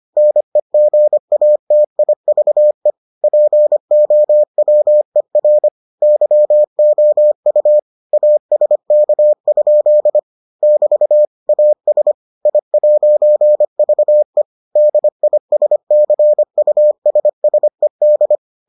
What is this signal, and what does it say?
Morse code, heard as a keyed tone, about 25 wpm, 600 Hz. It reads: NEGATIVE POWER YOU ASK? <BT> AS I'VE DISCUSSED